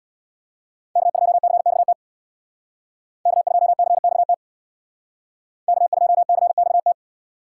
Morse code, heard as a keyed tone, about 59 wpm, 700 Hz. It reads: K2CXT K2CXT K2CXT
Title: Speed Racing - USA 1x3 Call Signs 50wpm